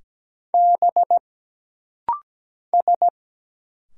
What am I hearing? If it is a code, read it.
Morse code, heard as a keyed tone, about 17 wpm, 700 Hz. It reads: B S
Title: Single Letter-Number Alt - Rapid-Fire 17wpm